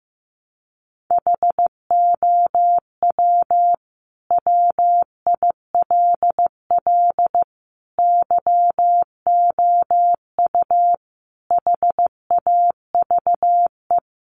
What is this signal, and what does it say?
Morse code, heard as a keyed tone, about 15 wpm, 700 Hz. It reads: HOW WILL YOU HAVE
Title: Sentences from Top 900 Words 15wpm